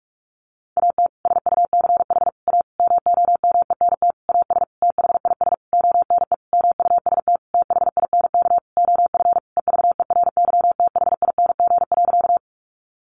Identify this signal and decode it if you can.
Morse code, heard as a keyed tone, about 44 wpm, 700 Hz. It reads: AT SUCH A MOMENT AS THIS ONE MUST THINK OF EVERYTHING.
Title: Sentences from Top 1000 Words 45wpm